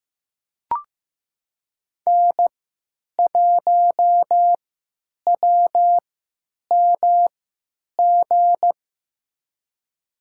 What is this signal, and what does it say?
Morse code, heard as a keyed tone, about 15 wpm, spaced as for 9 wpm, 700 Hz. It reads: N1WMG
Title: USA Call Signs - 3x Character Spacing - 15wpm